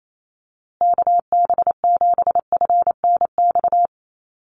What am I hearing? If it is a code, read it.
Morse code, heard as a keyed tone, about 28 wpm, 700 Hz. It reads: KB7FNX
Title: USA Call Signs Repeated 2x 28wpm